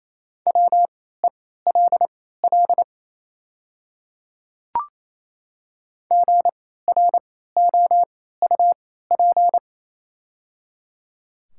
Morse code, heard as a keyed tone, about 28 wpm, spaced as for 16 wpm, 690 Hz. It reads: WELL GROUP
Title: 3x Character Spacing - Top 500 Words - 28wpm